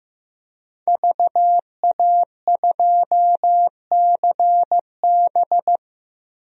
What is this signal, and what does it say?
Morse code, heard as a keyed tone, about 15 wpm, 700 Hz. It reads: VA2CB
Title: DXCC-100 Call Signs - 5 Characters - Repeated 2x - 15wpm